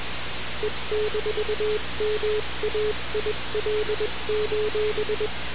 Morse code, heard as a keyed tone, about 21 wpm, 420 Hz, with noise. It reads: E-MAIL:
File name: radiocap / REC z